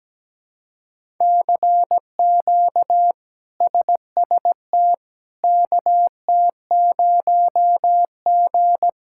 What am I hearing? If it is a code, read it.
Morse code, heard as a keyed tone, about 17 wpm, 700 Hz. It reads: CQ SST KT0G
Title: K1USN SST QSO Practice - 17wpm